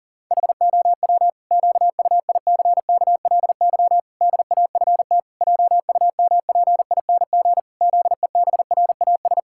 Morse code, heard as a keyed tone, about 40 wpm, 700 Hz. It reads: HOW QUICKLY DAFT JUMPING ZEBRAS
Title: Pangram Challenge v2 - 40wpm